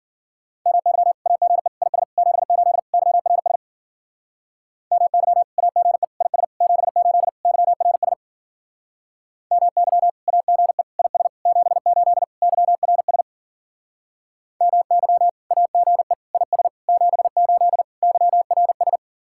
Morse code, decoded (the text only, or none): MY AGE IS 78 YRS MY AGE IS 78 YRS MY AGE IS 78 YRS MY AGE IS 78 YRS